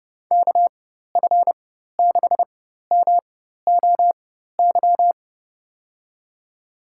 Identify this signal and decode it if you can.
Morse code, heard as a keyed tone, about 30 wpm, spaced as for 14 wpm, 700 Hz. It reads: KF6MOY
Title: USA Call Signs - 6 Characters - 4x Character Spacing - 30wpm